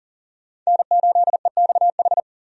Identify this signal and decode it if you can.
Morse code, heard as a keyed tone, about 40 wpm, 700 Hz. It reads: N8EXH
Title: USA Call Signs - 5 Characters - 40wpm